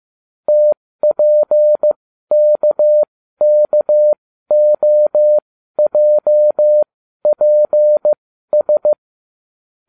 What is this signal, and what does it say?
Morse code, heard as a keyed tone, about 15 wpm, spaced as for 12 wpm, 600 Hz. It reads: TPKKOJPS